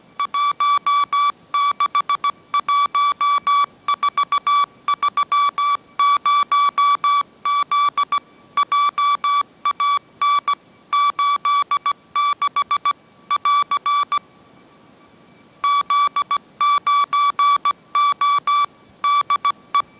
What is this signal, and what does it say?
Morse code, heard as a keyed tone, about 17 wpm, 1150 Hz. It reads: 161430Z JAN 86 <AR> Z9O DE
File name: mrx / mrx080 18wpm